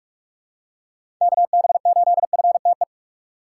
Morse code, heard as a keyed tone, about 46 wpm, 700 Hz. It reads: KB8FTE
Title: USA Call Signs - 6 Characters - 3x Character Spacing - 45wpm